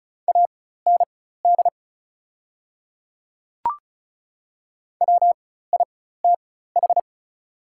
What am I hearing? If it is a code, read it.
Morse code, heard as a keyed tone, about 36 wpm, spaced as for 16 wpm, 700 Hz. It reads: AND WITH
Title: 4x Character Spacing - Top 200 Words - 35wpm